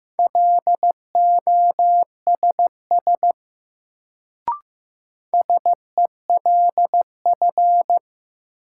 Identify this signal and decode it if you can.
Morse code, heard as a keyed tone, about 15 wpm, 700 Hz. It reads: LOSS SELF
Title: Sentences from Top 1400 Words - New Word Drill Alt - 15wpm